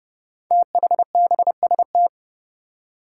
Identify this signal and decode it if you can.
Morse code, heard as a keyed tone, about 28 wpm, 700 Hz. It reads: THBST
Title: Code Groups - 5 Letters - 30wpm